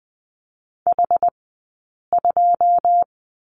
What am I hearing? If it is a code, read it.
Morse code, heard as a keyed tone, about 20 wpm, 700 Hz. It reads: H 2
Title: 080 Lesson 14d - Number 2 - Two Characters 20wpm